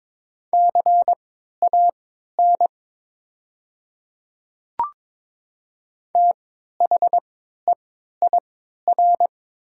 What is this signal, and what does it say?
Morse code, heard as a keyed tone, about 22 wpm, spaced as for 13 wpm, 700 Hz. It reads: CAN THEIR
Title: 3x Character Spacing - Top 100 Words - 22wpm